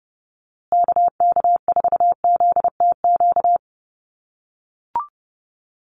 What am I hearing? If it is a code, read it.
Morse code, heard as a keyed tone, about 30 wpm, 700 Hz. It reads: KK4ZTQ
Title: USA Call Signs Repeated 3x 30wpm